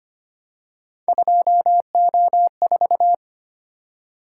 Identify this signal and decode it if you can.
Morse code, heard as a keyed tone, about 29 wpm, 700 Hz. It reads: 2O4
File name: Speed-Racing - Three Letter-Number Combinations - 25wpm